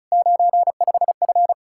Morse code, heard as a keyed tone, about 42 wpm, 700 Hz. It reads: 95F
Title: Speed-Racing Common QSO Phrases - 35wpm